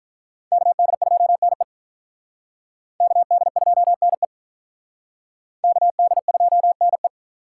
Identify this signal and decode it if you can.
Morse code, heard as a keyed tone, about 45 wpm, 700 Hz. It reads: KD2NE KD2NE KD2NE